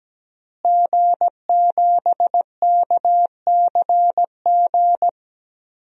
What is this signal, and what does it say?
Morse code, heard as a keyed tone, about 17 wpm, 700 Hz. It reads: G7KCG
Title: DXCC-100 Call Signs - 5 Characters - Repeated 3x - 17wpm